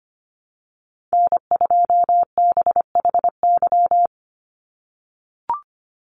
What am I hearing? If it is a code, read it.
Morse code, heard as a keyed tone, about 25 wpm, 690 Hz. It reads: N2BHY
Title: USA Call Signs 25wpm